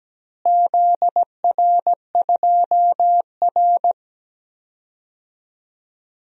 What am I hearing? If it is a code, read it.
Morse code, heard as a keyed tone, about 17 wpm, 700 Hz. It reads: ZR2R
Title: DXCC-100 Call Signs - 4 Characters - Repeated 3x - 17wpm